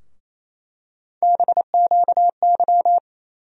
Morse code, heard as a keyed tone, about 28 wpm, 700 Hz. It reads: BQY
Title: USA Call Sign Postfixes - 3 Letters 28wpm